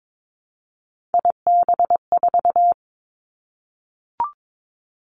Morse code, heard as a keyed tone, about 22 wpm, 700 Hz. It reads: IB4
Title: Three Challenging Letter-Number Combinations 22wpm